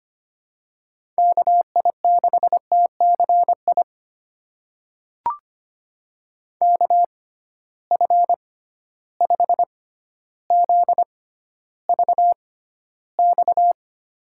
Morse code, heard as a keyed tone, about 25 wpm, spaced as for 13 wpm, 700 Hz. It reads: KI6TCI KF5ZVX
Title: USA Call Signs - 6 Characters - 6x Character Spacing - 25wpm